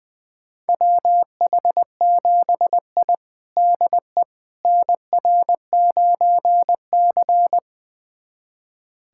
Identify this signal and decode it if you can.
Morse code, heard as a keyed tone, about 20 wpm, 700 Hz. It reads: WH7I DE NR9C
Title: USA Callsign DE Callsign - 4 Characters - 20wpm